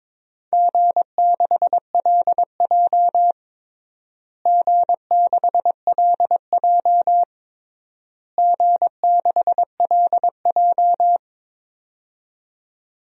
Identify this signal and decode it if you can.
Morse code, heard as a keyed tone, about 22 wpm, 700 Hz. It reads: G6LJ G6LJ G6LJ
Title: DXCC-100 Call Signs - 4 Characters - Repeated 3x - 22wpm